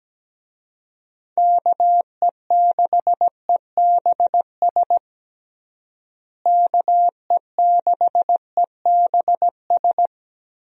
Morse code, heard as a keyed tone, about 17 wpm, 700 Hz. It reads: KE6EBS KE6EBS
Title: USA Call Signs - 6 Characters - Repeated 3x - 17wpm